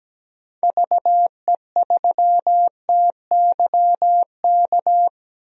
Morse code, heard as a keyed tone, about 17 wpm, 700 Hz. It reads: VE3TYK
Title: DXCC-100 Call Signs - 6 Characters - Repeated 3x - 17wpm